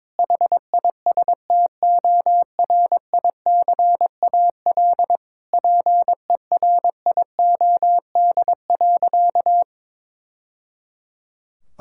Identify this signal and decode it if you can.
Morse code, heard as a keyed tone, about 21 wpm, 700 Hz. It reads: HISTORICAL PERIOD.
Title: Sentences from Top 1000 Words - Review Part 5 - 22wpm